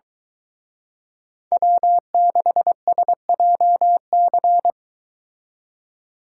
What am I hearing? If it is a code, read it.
Morse code, heard as a keyed tone, about 23 wpm, 700 Hz. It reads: W6SJC